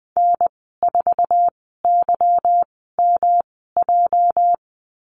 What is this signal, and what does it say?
Morse code, heard as a keyed tone, about 20 wpm, spaced as for 14 wpm, 700 Hz. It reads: N4YMJ
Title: 238 Lesson 38g - Letter q - Call Signs 20wpm